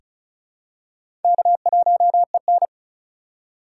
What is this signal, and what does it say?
Morse code, heard as a keyed tone, about 36 wpm, 700 Hz. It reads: K1EN
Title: USA Call Signs - 2x Character Spacing - 35wpm